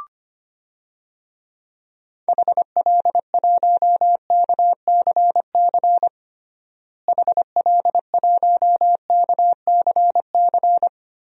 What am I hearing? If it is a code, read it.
Morse code, heard as a keyed tone, about 25 wpm, 700 Hz. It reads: HL1KCC HL1KCC
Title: DXCC-100 Call Signs - 6 Characters - Repeated 3x - 25wpm